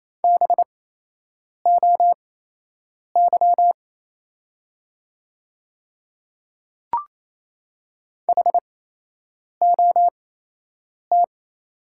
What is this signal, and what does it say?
Morse code, heard as a keyed tone, about 28 wpm, spaced as for 8 wpm, 700 Hz. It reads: BOY HOT
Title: 8x Character Spacing - Top 200 Words - 28wpm